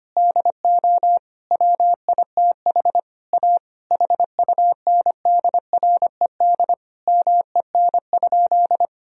Morse code, heard as a keyed tone, about 25 wpm, 700 Hz. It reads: DO WITH A HUNDRED MEN?
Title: Sentences from Top 1000 Words - Review Part 2 - 25wpm